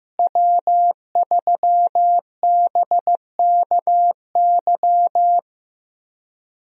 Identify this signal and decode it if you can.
Morse code, heard as a keyed tone, about 15 wpm, 700 Hz. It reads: W3BKY
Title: USA Call Signs - 5 Characters - Repeated 2x - 15wpm